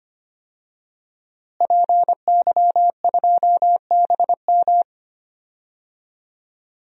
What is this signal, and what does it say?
Morse code, heard as a keyed tone, about 25 wpm, 700 Hz. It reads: PY2BM
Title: DXCC-100 Call Signs - 5 Characters - Repeated 3x - 25wpm